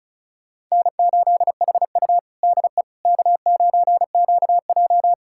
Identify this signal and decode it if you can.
Morse code, heard as a keyed tone, about 34 wpm, 700 Hz. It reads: N8HU DE K9QJ
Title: USA Callsign DE Callsign - 4 Characters - 35wpm